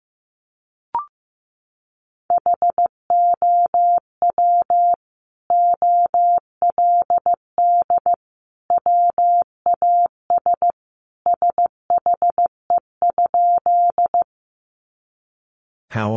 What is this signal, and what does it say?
Morse code, heard as a keyed tone, about 15 wpm, 700 Hz. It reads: HOW OLD WAS SHE?
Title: Sentences from Top 200 Words 15wpm